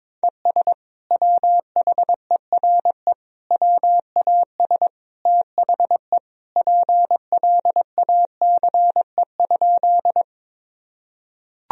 Morse code, heard as a keyed tone, about 22 wpm, 700 Hz. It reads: ES WHERE WAS THE PLACE?
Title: Sentences from Top 200 Words and Abbreviations 22wpm